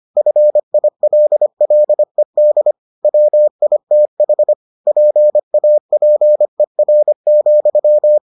Morse code, heard as a keyed tone, about 25 wpm, 600 Hz. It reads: FILLED WITH PAPER,